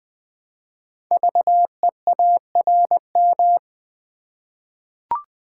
Morse code, heard as a keyed tone, about 20 wpm, 700 Hz. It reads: VEARM
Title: Code Groups - 5 Letters - 20wpm